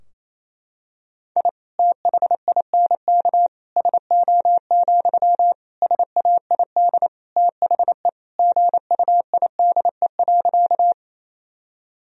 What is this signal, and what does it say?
Morse code, heard as a keyed tone, about 28 wpm, 700 Hz. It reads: I THINK SO, SAID THE GUIDE.
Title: Sentences from Top 1000 Words - Review Part 3 - 28wpm